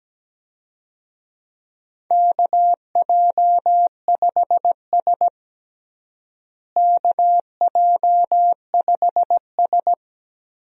Morse code, heard as a keyed tone, about 17 wpm, 700 Hz. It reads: KJ5S KJ5S
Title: USA Call Signs - 4 Characters - Repeated 2x - 17wpm